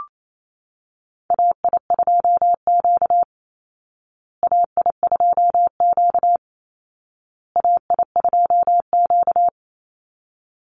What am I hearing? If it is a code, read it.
Morse code, heard as a keyed tone, about 28 wpm, 700 Hz. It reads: AI2Q AI2Q AI2Q
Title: Call Signs Repeated 2x - 28wpm